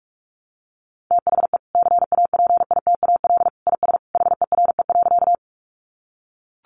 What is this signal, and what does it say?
Morse code, heard as a keyed tone, about 44 wpm, 700 Hz. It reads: THE CAPITAL IS HERE.
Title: Sentences from Top 1000 Words 45wpm